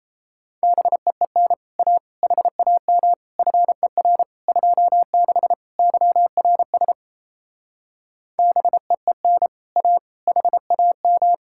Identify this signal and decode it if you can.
Morse code, heard as a keyed tone, about 37 wpm, 700 Hz. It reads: BEEN A HAM FER 26 YRS BEEN A HAM